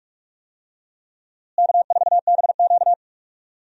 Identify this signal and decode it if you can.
Morse code, heard as a keyed tone, about 44 wpm, 700 Hz. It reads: K4BQ
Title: USA Call Signs - 4 Characters - Repeated 2x - 45wpm